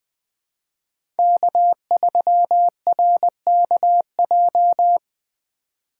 Morse code, heard as a keyed tone, about 20 wpm, 700 Hz. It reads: K3RKJ
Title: USA Call Signs - 5 Characters - 3x Character Spacing - 20wpm